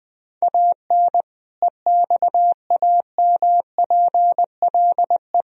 Morse code, heard as a keyed tone, about 21 wpm, 700 Hz. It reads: AN EXAMPLE
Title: Sentences from Top 400 Words and Abbreviations 20wpm